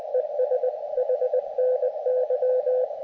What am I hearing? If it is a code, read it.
Morse code, heard as a keyed tone, about 20 wpm, 510 Hz, with radio noise. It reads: ES HNY
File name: Expedition / SWL UT5NM 20.12.2012 VANGUARD UA1CEG p IT9IFI